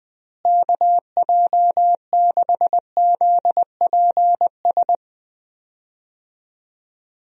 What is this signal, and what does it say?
Morse code, heard as a keyed tone, about 20 wpm, 700 Hz. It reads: KJ6ZPS
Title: USA Call Signs - 6 Characters - Repeated 2x - 20wpm